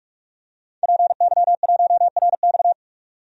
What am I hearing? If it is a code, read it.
Morse code, heard as a keyed tone, about 46 wpm, 700 Hz. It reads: PY1RX
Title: DXCC-100 Call Signs - 5 Characters - 45wpm